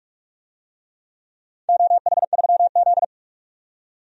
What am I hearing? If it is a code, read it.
Morse code, heard as a keyed tone, about 46 wpm, 700 Hz. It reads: OH3Z
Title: DXCC-100 Call Signs - 4 Characters - 45wpm